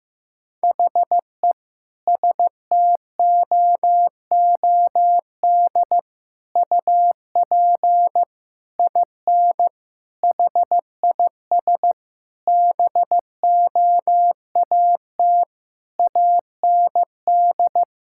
Morse code, heard as a keyed tone, about 15 wpm, 700 Hz. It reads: HE STOOD UP IN HIS BOAT AND
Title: Sentences from Top 1600 Words 15wpm